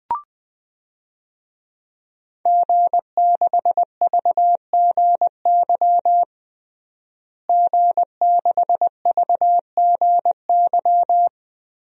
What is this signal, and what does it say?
Morse code, heard as a keyed tone, about 21 wpm, 700 Hz. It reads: G6VGY G6VGY
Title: DXCC-100 Call Signs - 5 Characters - Repeated 2x - 20wpm